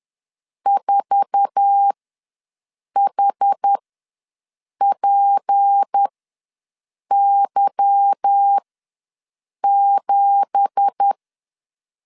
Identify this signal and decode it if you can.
Morse code, heard as a keyed tone, about 10 wpm, spaced as for 6 wpm, 800 Hz. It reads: 4HPY7
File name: LEZIONI / 9 2867 ws 10wpm 4215